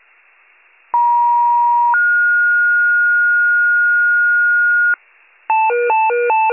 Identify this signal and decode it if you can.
Two-tone page, heard as a one-second tone, then a three-second tone, about 950 then 1430 Hz, with noise. It is heard from 0.9 to 4.9 s.